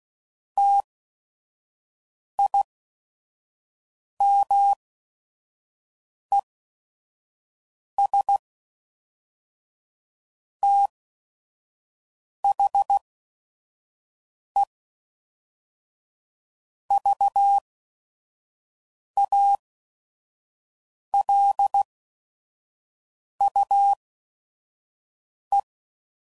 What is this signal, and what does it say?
Morse code, heard as a keyed tone, about 16 wpm, spaced as for 5 wpm, 800 Hz. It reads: TIMES THE VALUE
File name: CW Code Groups @5WPM / Technical Subjects / XL